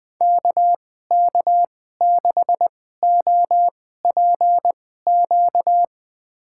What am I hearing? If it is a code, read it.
Morse code, heard as a keyed tone, about 20 wpm, spaced as for 15 wpm, 700 Hz. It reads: KK6OPQ
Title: USA Call Signs - 6 Characters - 2x Character Spacing - 20wpm